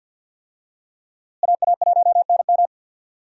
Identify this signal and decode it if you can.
Morse code, heard as a keyed tone, about 53 wpm, 700 Hz. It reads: AA1NM
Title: USA Call Signs - 5 Characters - Repeated 3x - 50wpm